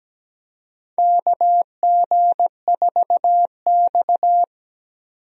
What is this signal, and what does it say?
Morse code, heard as a keyed tone, about 17 wpm, 700 Hz. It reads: KG4X
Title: USA Call Signs - 4 Characters - 17wpm